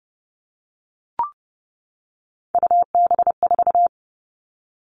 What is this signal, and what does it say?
Morse code, heard as a keyed tone, about 30 wpm, 700 Hz. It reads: UB4
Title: Three Challenging Letter-Number Combinations 30wpm